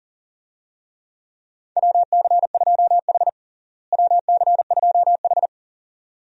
Morse code, heard as a keyed tone, about 37 wpm, 700 Hz. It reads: WC2H WC2H
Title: USA Call Signs - 4 Characters - Repeated 2x - 40wpm